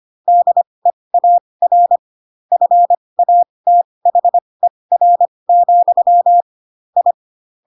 Morse code, heard as a keyed tone, about 25 wpm, spaced as for 20 wpm, 700 Hz. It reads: DEAR FATHER, I
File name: Pinocchio / Pinocchio ch7 15 25 ik1hge